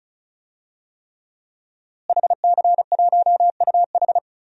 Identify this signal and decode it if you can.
Morse code, heard as a keyed tone, about 36 wpm, 700 Hz. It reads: HC1UH
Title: DXCC-100 Call Signs - 5 Characters - Repeated 3x - 35wpm